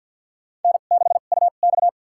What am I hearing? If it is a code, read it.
Morse code, heard as a keyed tone, about 53 wpm, spaced as for 37 wpm, 700 Hz. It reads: N6UX
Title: USA Call Signs - 4 Characters - 2x Character Spacing - 50wpm